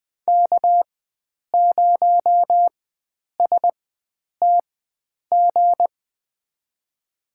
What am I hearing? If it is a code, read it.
Morse code, heard as a keyed tone, about 20 wpm, spaced as for 9 wpm, 700 Hz. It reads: K0STG
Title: USA Call Signs - 5 Characters - 4x Character Spacing - 20wpm